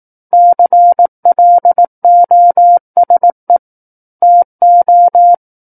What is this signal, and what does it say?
Morse code, heard as a keyed tone, about 18 wpm, 700 Hz. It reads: CLOSE TO
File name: 18 WPM